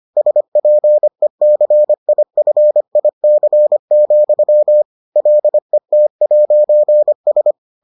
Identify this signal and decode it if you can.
Morse code, heard as a keyed tone, about 25 wpm, 600 Hz. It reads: SPECIFIC, LET'S